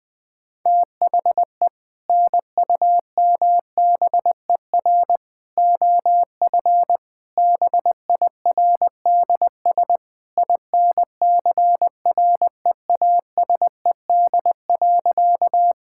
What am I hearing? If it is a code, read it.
Morse code, heard as a keyed tone, about 20 wpm, 700 Hz. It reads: THE NUMBER OF BIRDS INCREASED.
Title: Sentences with 3-Word Phrases - 20wpm